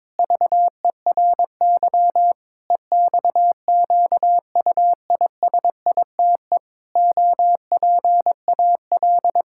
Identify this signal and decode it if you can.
Morse code, heard as a keyed tone, about 22 wpm, 700 Hz. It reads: VERY EXQUISITE OPAL
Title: Pangram Challenge v2 - 22wpm